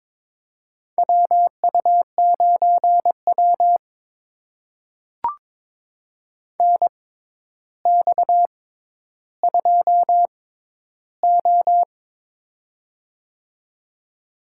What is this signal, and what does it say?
Morse code, heard as a keyed tone, about 22 wpm, spaced as for 11 wpm, 700 Hz. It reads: WU9W NX2O
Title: USA Call Signs - 4 Characters - 6x Character Spacing - 22wpm